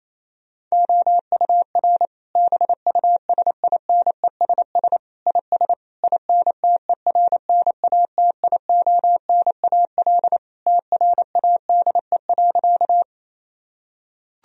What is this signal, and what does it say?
Morse code, heard as a keyed tone, about 28 wpm, 700 Hz. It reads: OUR BUSINESS IS INTERNATIONAL TRADE.
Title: Sentences from Top 1000 Words - Review Part 4 - 28wpm